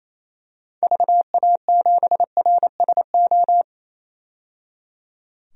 Morse code, heard as a keyed tone, about 27 wpm, 700 Hz. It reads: VA7RSO
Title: DXCC-100 Call Signs - 6 Characters - 28wpm